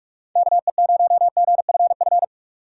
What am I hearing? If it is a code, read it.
Morse code, heard as a keyed tone, about 41 wpm, 700 Hz. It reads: KE0GFF
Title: USA Call Signs - 6 Characters - 3x Character Spacing - 45wpm